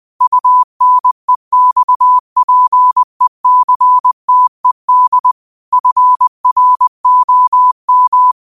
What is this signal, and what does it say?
Morse code, heard as a keyed tone, about 20 wpm, 1000 Hz. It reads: UNEXPECTED FROM
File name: qotdfiles / 20wpm